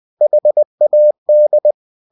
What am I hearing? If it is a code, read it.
Morse code, heard as a keyed tone, about 20 wpm, 600 Hz. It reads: HAD
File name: cw-books / winnie-the-pooh / winnie-20wpm-page-0107